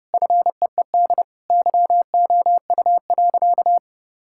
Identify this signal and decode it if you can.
Morse code, heard as a keyed tone, about 31 wpm, 700 Hz. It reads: FEED YOU.
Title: Sentences from Top 1600 Words 30wpm